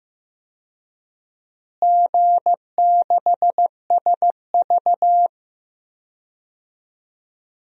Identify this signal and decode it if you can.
Morse code, heard as a keyed tone, about 15 wpm, 700 Hz. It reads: G6SV
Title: DXCC-100 Call Signs - 4 Characters - Repeated 2x - 15wpm